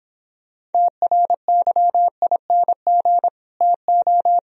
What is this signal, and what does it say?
Morse code, heard as a keyed tone, about 29 wpm, 700 Hz. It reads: TRYING TO